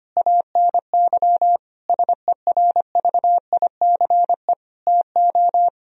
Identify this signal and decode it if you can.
Morse code, heard as a keyed tone, about 25 wpm, 700 Hz. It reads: ANY SERVICE TO
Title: Sentences from Top 1000 Words - Review Part 4 - 25wpm